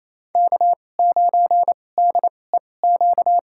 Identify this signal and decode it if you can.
Morse code, heard as a keyed tone, about 28 wpm, spaced as for 20 wpm, 700 Hz. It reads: K9DEQ
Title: USA Call Signs - 5 Characters - 2x Character Spacing - 28wpm